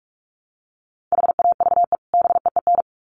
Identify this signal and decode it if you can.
Morse code, heard as a keyed tone, about 44 wpm, 700 Hz. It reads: HAVE BEEN
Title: Sets of 2 Words 45wpm